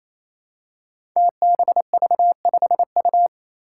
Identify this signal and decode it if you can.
Morse code, heard as a keyed tone, about 28 wpm, 700 Hz. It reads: TBV5U
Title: Five Challenging Letter-Number Combinations 28wpm